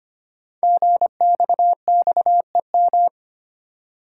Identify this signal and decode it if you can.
Morse code, heard as a keyed tone, about 25 wpm, 700 Hz. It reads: GXXEM
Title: Code Groups - 5 Letters - 25wpm